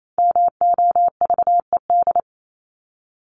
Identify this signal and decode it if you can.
Morse code, heard as a keyed tone, about 28 wpm, 700 Hz. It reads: MOVED
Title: ED Suffix - Top 100 Words with Base Words 28wpm